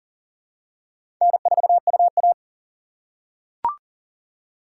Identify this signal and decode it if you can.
Morse code, heard as a keyed tone, about 40 wpm, 700 Hz. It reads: N4UA
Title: USA Call Signs - 4 Characters - 3x Character Spacing - 40wpm